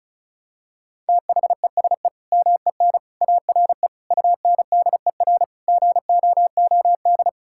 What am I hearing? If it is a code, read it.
Morse code, heard as a keyed tone, about 36 wpm, 700 Hz. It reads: THESE MEN ARE UNDER GOOD
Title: Sentences from Top 1000 Words - Review Part 6 - 35wpm